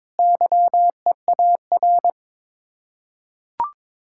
Morse code, heard as a keyed tone, about 22 wpm, 700 Hz. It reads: YEAR
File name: practice / 22wpm / Sets of 8 Words - 22wpm